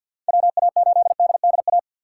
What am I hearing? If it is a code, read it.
Morse code, heard as a keyed tone, about 48 wpm, 700 Hz. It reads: WA8DDA
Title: USA Call Signs - 6 Characters - 4x Character Spacing - 50wpm